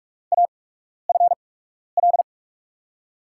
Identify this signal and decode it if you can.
Morse code, heard as a keyed tone, about 44 wpm, spaced as for 12 wpm, 700 Hz. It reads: AFL
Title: USA Call Signs - 6 Characters - 8x Character Spacing - 45wpm